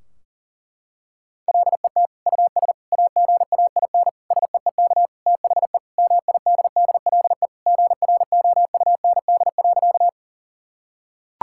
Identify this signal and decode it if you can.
Morse code, heard as a keyed tone, about 40 wpm, 700 Hz. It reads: LET US AGAIN SEEK THE MIDDLE GROUND.
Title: Sentences from Top 1600 Words 40wpm